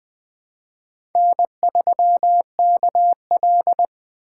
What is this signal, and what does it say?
Morse code, heard as a keyed tone, about 21 wpm, 700 Hz. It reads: N3KL
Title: USA Call Signs - 3x Character Spacing - 20wpm